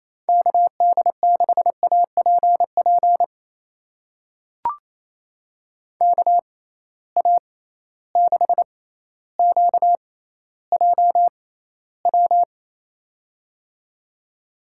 Morse code, heard as a keyed tone, about 28 wpm, spaced as for 14 wpm, 700 Hz. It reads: KD6APP KA6QJW
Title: USA Call Signs - 6 Characters - 6x Character Spacing - 28wpm